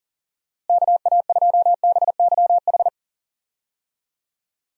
Morse code, heard as a keyed tone, about 40 wpm, 700 Hz. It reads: KA2BYH
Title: USA Call Signs - 6 Characters - Repeated 2x - 40wpm